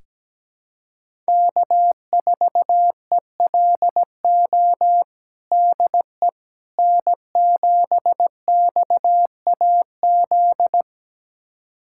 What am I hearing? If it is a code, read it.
Morse code, heard as a keyed tone, about 17 wpm, 700 Hz. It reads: K4ELO DE N7XAZ
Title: USA Callsign DE Callsign 17wpm